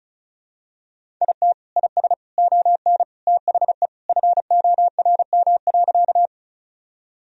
Morse code, heard as a keyed tone, about 36 wpm, 700 Hz. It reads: IT IS ON THE FORM.
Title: Sentences from Top 1000 Words - Review Part 1 - 35wpm